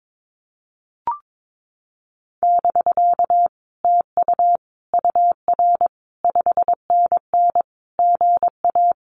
Morse code, heard as a keyed tone, about 22 wpm, 700 Hz. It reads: <BK> TU UR 5NN GA
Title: Common QSO Phrases Encore 22wpm